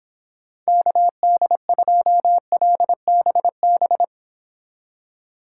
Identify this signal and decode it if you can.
Morse code, heard as a keyed tone, about 26 wpm, 700 Hz. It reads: KD2LBB